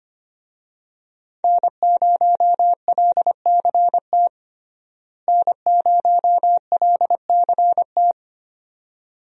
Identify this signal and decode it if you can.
Morse code, heard as a keyed tone, about 25 wpm, 700 Hz. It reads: N0LCT N0LCT
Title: USA Call Signs - 5 Characters - Repeated 2x - 25wpm